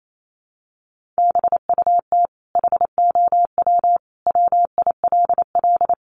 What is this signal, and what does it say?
Morse code, heard as a keyed tone, about 28 wpm, 700 Hz. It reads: BUT HOW WILL
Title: Sentences from Top 1100 Words 28wpm